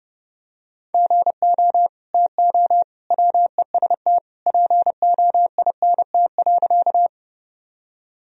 Morse code, heard as a keyed tone, about 30 wpm, 700 Hz. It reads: GO TO WEST POINT.
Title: Sentences from Top 1000 Words - Review Part 2 - 30wpm